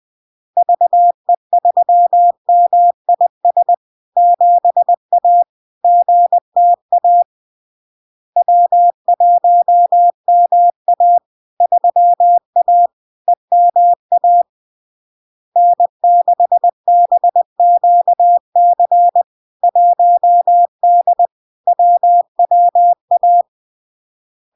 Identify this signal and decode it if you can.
Morse code, heard as a keyed tone, about 20 wpm, 700 Hz. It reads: VE3MIS 7A GTA W1MA 3A EMA N6BQC 1D WWA
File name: LARK-FD-2022-Log-in-CW-20wpm-700-Hz